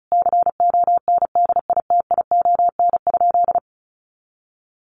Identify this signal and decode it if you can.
Morse code, heard as a keyed tone, about 34 wpm, 700 Hz. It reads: CONDITION?
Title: Sentences from Top 1000 Words 35wpm